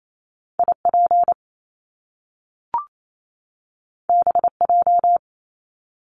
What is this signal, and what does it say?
Morse code, heard as a keyed tone, about 28 wpm, 700 Hz. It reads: IP BJ
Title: Top Two Letter Combinations in Words 28wpm